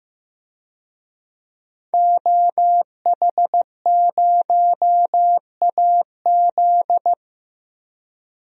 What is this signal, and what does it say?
Morse code, heard as a keyed tone, about 16 wpm, 700 Hz. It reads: OH0AZ
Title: DXCC-100 Call Signs - 5 Characters - Repeated 3x - 15wpm